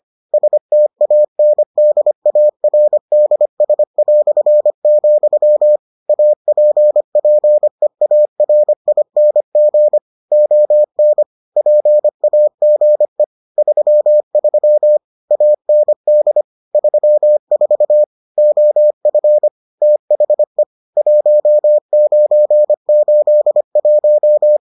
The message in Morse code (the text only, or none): STANDARDS", APPEARING ON PAGE 33 AND 34 OF THE 1981